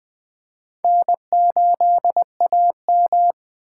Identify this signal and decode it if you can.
Morse code, heard as a keyed tone, about 24 wpm, 700 Hz. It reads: N8AM
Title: Speed Racing - USA 1x2 Call Signs 20wpm